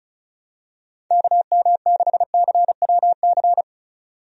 Morse code, heard as a keyed tone, about 34 wpm, 700 Hz. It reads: KM6CWC
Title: USA Call Signs - 6 Characters - 35wpm